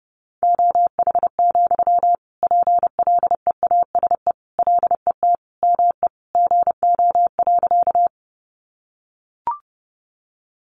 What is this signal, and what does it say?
Morse code, heard as a keyed tone, about 30 wpm, 700 Hz. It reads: OH, PLEASE LET ME GO.
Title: Sentences from Top 1500 Words 30wpm